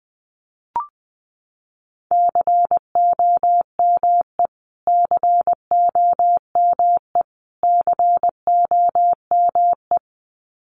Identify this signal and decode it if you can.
Morse code, heard as a keyed tone, about 19 wpm, 700 Hz. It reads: COME COME COME
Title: Top 500 Words Repeated 20wpm